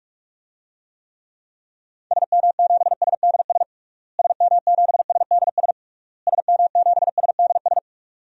Morse code, heard as a keyed tone, about 46 wpm, 700 Hz. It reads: SM7SDS SM7SDS SM7SDS
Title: DXCC-100 Call Signs - 6 Characters - Repeated 3x - 45wpm